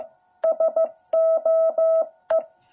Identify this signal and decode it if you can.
Morse code, heard as a keyed tone, about 15 wpm, 660 Hz. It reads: SOE
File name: jh8cbh / 2021cw